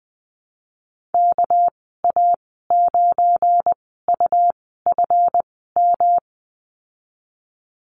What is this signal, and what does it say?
Morse code, heard as a keyed tone, about 20 wpm, spaced as for 14 wpm, 700 Hz. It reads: KA9UFM
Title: 245 Lesson 39g - Letter z - Call Signs 20wpm